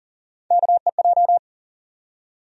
Morse code, heard as a keyed tone, about 40 wpm, 700 Hz. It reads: KEJ
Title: USA Call Sign Postfixes - 3 Letters 40wpm